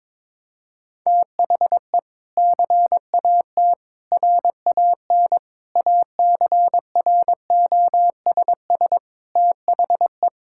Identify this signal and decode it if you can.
Morse code, heard as a keyed tone, about 23 wpm, 700 Hz. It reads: THE CAT RAN ACROSS THE
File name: practice / 22wpm / Sentences with 2-Word Phrases - 22wpm